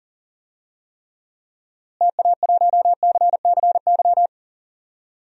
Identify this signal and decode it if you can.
Morse code, heard as a keyed tone, about 39 wpm, 700 Hz. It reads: TA1CCY
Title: DXCC-100 Call Signs - 6 Characters - Repeated 2x - 40wpm